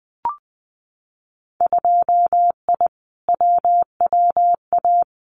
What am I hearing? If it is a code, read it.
Morse code, heard as a keyed tone, about 20 wpm, 700 Hz. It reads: 2I WWA
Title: Winter Field Day Exchanges Encore 20wpm